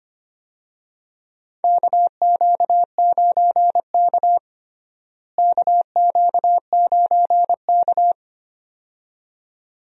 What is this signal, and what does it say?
Morse code, heard as a keyed tone, about 25 wpm, 700 Hz. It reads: KQ9K KQ9K
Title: USA Call Signs - 4 Characters - Repeated 2x - 25wpm